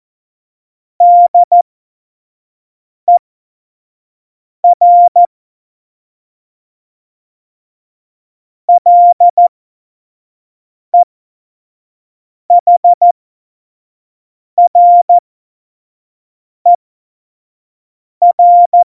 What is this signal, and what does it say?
Morse code, heard as a keyed tone, about 14 wpm, spaced as for 5 wpm, 700 Hz. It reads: DER LEHRER
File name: Morsekurs / Witze / Witz 02. April 2019, 25 F70